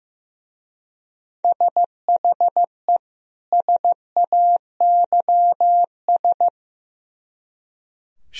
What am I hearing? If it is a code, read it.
Morse code, heard as a keyed tone, about 16 wpm, 700 Hz. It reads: SHE SAYS
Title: Speed-Racing - Sets of 2 Words 15wpm